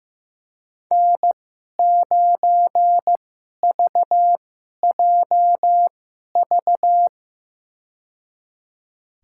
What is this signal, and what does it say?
Morse code, heard as a keyed tone, about 15 wpm, spaced as for 11 wpm, 700 Hz. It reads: N9VJV
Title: USA Call Signs - 5 Characters - 2x Character Spacing - 15wpm